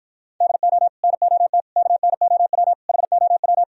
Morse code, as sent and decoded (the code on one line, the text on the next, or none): -.. --- / -. --- - / -.- -. --- .-- / .... --- .--
DO NOT KNOW HOW